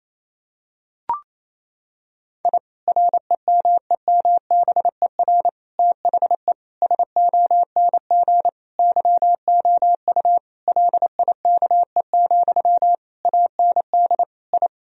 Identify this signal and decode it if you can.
Morse code, heard as a keyed tone, about 28 wpm, 700 Hz. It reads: I REMEMBER THE SONG YOU LIKE, AND I
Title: Sentences from Top 1000 Words - Review Part 2 - 28wpm